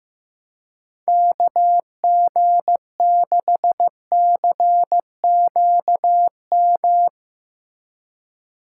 Morse code, heard as a keyed tone, about 16 wpm, 700 Hz. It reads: KG6CQM
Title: USA Call Signs - 6 Characters - 4x Character Spacing - 15wpm